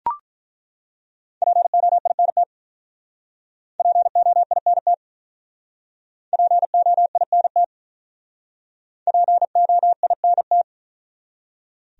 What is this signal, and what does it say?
Morse code, heard as a keyed tone, about 42 wpm, 700 Hz. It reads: POINT POINT POINT POINT
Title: Speed-Racing Top 500 Words 35wpm